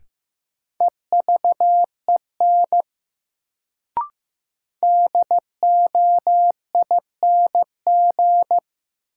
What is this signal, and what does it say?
Morse code, heard as a keyed tone, about 15 wpm, 700 Hz. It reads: EVEN DOING
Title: Sets of 4 Words - New Word Drill - 15wpm